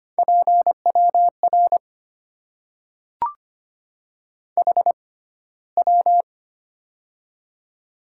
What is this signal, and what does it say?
Morse code, heard as a keyed tone, about 25 wpm, spaced as for 15 wpm, 700 Hz. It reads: PWR HW
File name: Top 90 QSO Elements - 6x Character Spacing - 25wpm